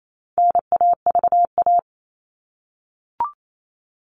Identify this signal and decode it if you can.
Morse code, heard as a keyed tone, about 28 wpm, 700 Hz. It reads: NAVA
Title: All Cut Numbers - 4-Digit 28wpm